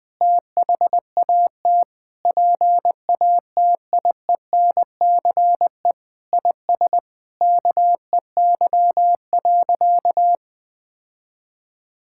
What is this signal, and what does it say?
Morse code, heard as a keyed tone, about 20 wpm, 700 Hz. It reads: THAT PATIENCE IS KEY.
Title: Sentences with 7-Word Phrases - 20wpm